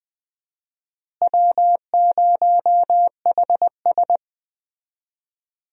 Morse code, heard as a keyed tone, about 20 wpm, 700 Hz. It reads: W0HS
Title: USA Call Signs - 4 Characters - Repeated 2x - 20wpm